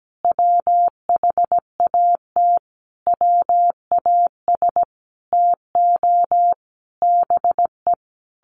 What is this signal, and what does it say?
Morse code, heard as a keyed tone, about 17 wpm, 700 Hz. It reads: WHAT WAS TO BE
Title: Sentences from Top 1200 Words 17wpm